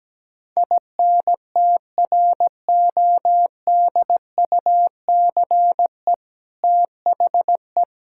Morse code, heard as a keyed tone, about 17 wpm, 700 Hz. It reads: INTRODUCE THE
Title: Sentences from Top 1600 Words 17wpm